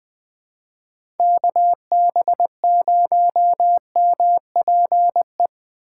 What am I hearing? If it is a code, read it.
Morse code, heard as a keyed tone, about 20 wpm, 700 Hz. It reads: KB0MPE
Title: USA Call Signs - 6 Characters - 3x Character Spacing - 20wpm